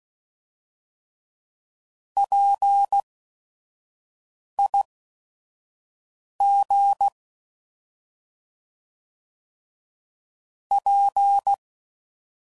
Morse code, heard as a keyed tone, about 16 wpm, spaced as for 5 wpm, 800 Hz. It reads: PIG P
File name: Prefixes / By First Letter / Ps